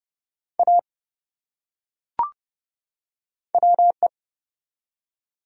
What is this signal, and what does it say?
Morse code, heard as a keyed tone, about 30 wpm, 700 Hz. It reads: A WE
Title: Top 100 Words 30wpm